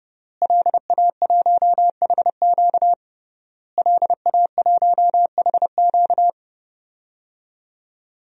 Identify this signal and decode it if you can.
Morse code, heard as a keyed tone, about 30 wpm, 700 Hz. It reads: LA1HQ LA1HQ
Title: DXCC-100 Call Signs - 5 Characters - Repeated 3x - 30wpm